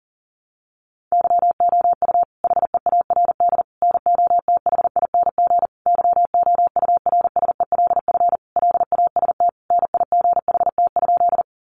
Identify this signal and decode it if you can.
Morse code, heard as a keyed tone, about 37 wpm, 700 Hz. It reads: YOU HEARD NOTHING YOURSELF LAST NIGHT?
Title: Sentences from Top 600 Words 40wpm